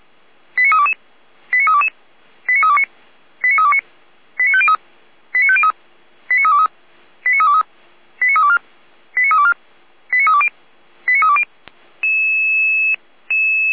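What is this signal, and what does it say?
Five-tone selective call (ZVEI), heard as quick runs of five tones, tones of about 70 ms. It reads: 89322 89322 89329 89329 89503 89503 89323 89323 89324 89324 89320 89320
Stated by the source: Five-tone ZVEI selective call received by SDR, from a Feuerwehr Orlamünde fire brigade alerting archive.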